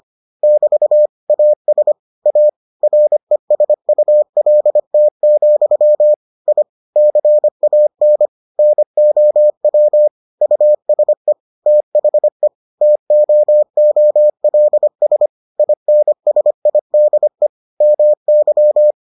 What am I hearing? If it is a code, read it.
Morse code, heard as a keyed tone, about 25 wpm, 600 Hz. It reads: <BT> AS A RESULT, I CAN NOW USE THE TOOLS INSIDE MY